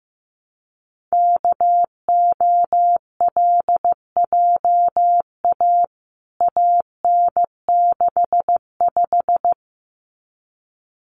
Morse code, heard as a keyed tone, about 15 wpm, 700 Hz. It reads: KOLJA AN65
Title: CWT Exchange 15wpm